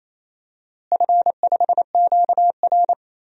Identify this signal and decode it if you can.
Morse code, heard as a keyed tone, about 28 wpm, 700 Hz. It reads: F5QR
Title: DXCC-100 Call Signs - 4 Characters - Repeated 2x - 28wpm